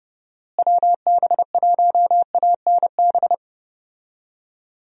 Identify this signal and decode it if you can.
Morse code, heard as a keyed tone, about 30 wpm, 700 Hz. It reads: WB1ANB
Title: USA Call Signs - 6 Characters - Repeated 3x - 30wpm